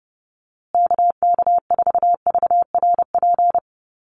Morse code, heard as a keyed tone, about 30 wpm, 700 Hz. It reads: KK4VRP
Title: USA Call Signs Repeated 3x 30wpm